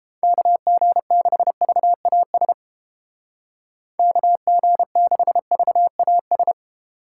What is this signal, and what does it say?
Morse code, heard as a keyed tone, about 32 wpm, 700 Hz. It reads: KG6VAS KG6VAS